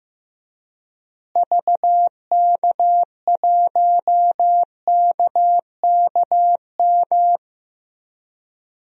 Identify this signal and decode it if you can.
Morse code, heard as a keyed tone, about 15 wpm, 700 Hz. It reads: VK1KKM
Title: DXCC-100 Call Signs - 6 Characters - Repeated 3x - 15wpm